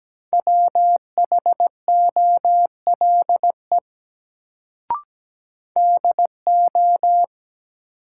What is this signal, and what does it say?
Morse code, heard as a keyed tone, about 17 wpm, 700 Hz. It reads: WHOLE DO
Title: Sets of 2 Words - New Word Drill - 17wpm